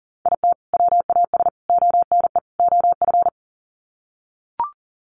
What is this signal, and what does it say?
Morse code, heard as a keyed tone, about 40 wpm, 700 Hz. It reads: IT WAS ONE OF
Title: Sets of 4 Words 40wpm